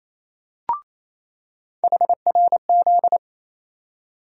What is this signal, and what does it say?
Morse code, heard as a keyed tone, about 28 wpm, 700 Hz. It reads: HRZ
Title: USA Call Sign Postfixes - 3 Letters 28wpm